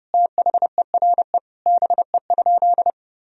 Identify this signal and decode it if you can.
Morse code, heard as a keyed tone, about 30 wpm, 700 Hz. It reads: THERE BE?
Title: Sentences from Top 1000 Words - Review Part 3 - 30wpm